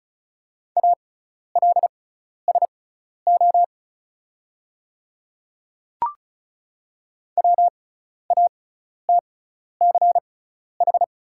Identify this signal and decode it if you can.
Morse code, heard as a keyed tone, about 36 wpm, spaced as for 12 wpm, 690 Hz. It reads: ALSO WATCH
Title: 6x Character Spacing - Top 300 Words - 35wpm